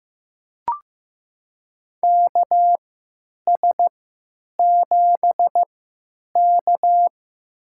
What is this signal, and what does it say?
Morse code, heard as a keyed tone, about 15 wpm, spaced as for 9 wpm, 700 Hz. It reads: KS7K
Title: USA Call Signs - 4 Characters - 3x Character Spacing - 15wpm